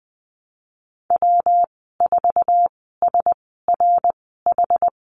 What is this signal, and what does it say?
Morse code, heard as a keyed tone, about 20 wpm, spaced as for 15 wpm, 700 Hz. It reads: W4SRH
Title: 167 Lesson 27g - Number 7 - Call Signs 20wpm